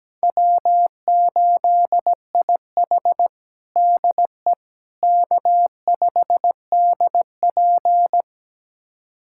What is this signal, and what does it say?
Morse code, heard as a keyed tone, about 17 wpm, 700 Hz. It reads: W8IH DE K5DP
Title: USA Callsign DE Callsign - 4 Characters - 17wpm